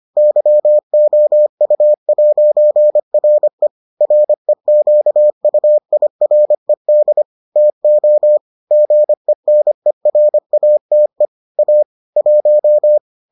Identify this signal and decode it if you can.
Morse code, heard as a keyed tone, about 25 wpm, 600 Hz. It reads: YOU'RE REQUIRED TO GENERATE A 1